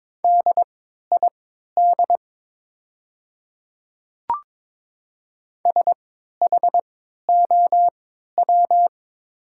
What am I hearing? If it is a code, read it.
Morse code, heard as a keyed tone, about 22 wpm, spaced as for 12 wpm, 700 Hz. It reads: DID SHOW
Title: 3x Character Spacing - Top 200 Words - 22wpm